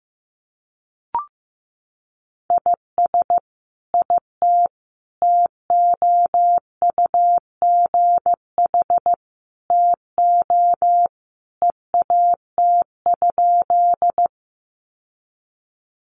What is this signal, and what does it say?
Morse code, heard as a keyed tone, about 15 wpm, 700 Hz. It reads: IS IT TOUGH TO EAT?
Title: Sentences from Top 1200 Words 15wpm